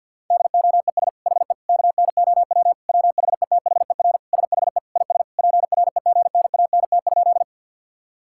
Morse code, heard as a keyed tone, about 50 wpm, 700 Hz. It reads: DOES HE KNOW WHETHER SHE IS PREGNANT?
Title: Sentences from Top 1000 Words - Review Part 6 - 50wpm